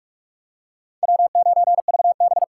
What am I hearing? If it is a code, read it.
Morse code, heard as a keyed tone, about 47 wpm, 700 Hz. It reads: W9VB
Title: USA Call Signs - 4x Character Spacing - 45wpm